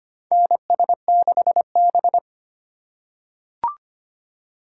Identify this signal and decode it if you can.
Morse code, heard as a keyed tone, about 25 wpm, 700 Hz. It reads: NS6B
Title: USA Call Signs - 4 Characters - 2x Character Spacing - 25wpm